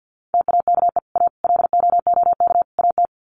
Morse code, heard as a keyed tone, about 50 wpm, 700 Hz. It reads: TAKE A LOOK AT
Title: Sets of 4 Words 50wpm